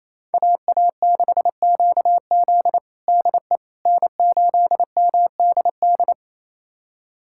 Morse code, heard as a keyed tone, about 27 wpm, 700 Hz. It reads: AA6QZ DE N8MDD
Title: USA Callsign DE Callsign - 5 Characters - 28wpm